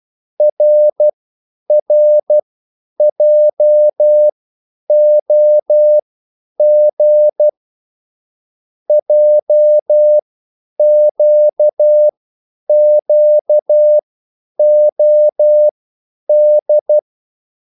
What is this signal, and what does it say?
Morse code, heard as a keyed tone, about 12 wpm, spaced as for 9 wpm, 600 Hz. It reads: RRJOG JQQOD
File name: QSL / CW lesson